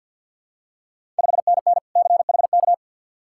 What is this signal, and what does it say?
Morse code, heard as a keyed tone, about 50 wpm, 700 Hz. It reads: 5NN CHK
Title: Ontario QSO Party Exchange 50wpm